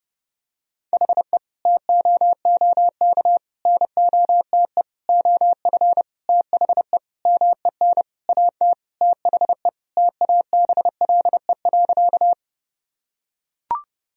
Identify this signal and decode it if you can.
Morse code, heard as a keyed tone, about 31 wpm, 700 Hz. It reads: HE TOOK NOTE OF THE MEN AT THE TABLE.
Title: Sentences from Top 1000 Words - Review Part 2 - 30wpm